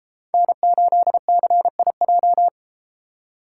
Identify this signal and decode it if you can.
Morse code, heard as a keyed tone, about 33 wpm, 700 Hz. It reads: N8CIJ